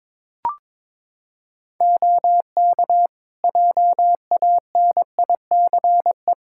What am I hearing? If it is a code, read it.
Morse code, heard as a keyed tone, about 22 wpm, 700 Hz. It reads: OK JANICE
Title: K1USN SST QSO Practice 22wpm